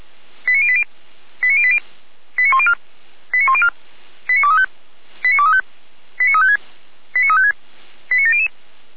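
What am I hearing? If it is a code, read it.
Five-tone selective call (ZVEI), heard as quick runs of five tones, tones of about 70 ms. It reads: 89080 89080 89104 89104 89235 89235 89356 89356 89700
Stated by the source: ZVEI five-tone selective call received on SDR.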